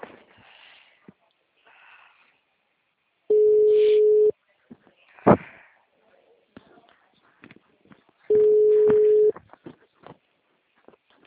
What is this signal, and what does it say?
Ringback tone (Europe, 425 Hz).